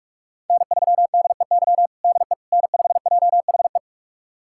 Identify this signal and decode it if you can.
Morse code, heard as a keyed tone, about 46 wpm, 700 Hz. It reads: N3DEY DE N5JHE
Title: USA Callsign DE Callsign - 5 Characters - 45wpm